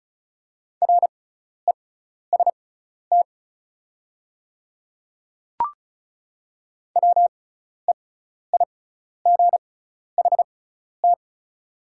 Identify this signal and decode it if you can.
Morse code, heard as a keyed tone, about 36 wpm, spaced as for 12 wpm, 700 Hz. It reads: REST WEIGHT
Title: 6x Character Spacing - Top 500 Words - 35wpm